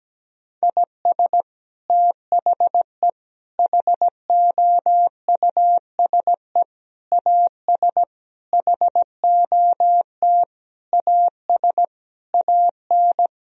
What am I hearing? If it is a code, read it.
Morse code, heard as a keyed tone, about 17 wpm, 700 Hz. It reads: IS THE HOUSE AS HOT AS AN
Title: Sentences from Top 1000 Words - Review Part 4 - 17wpm